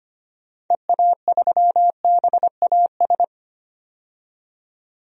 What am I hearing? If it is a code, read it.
Morse code, heard as a keyed tone, about 25 wpm, 700 Hz. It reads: EA3BAS
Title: DXCC-100 Call Signs - 6 Characters - 25wpm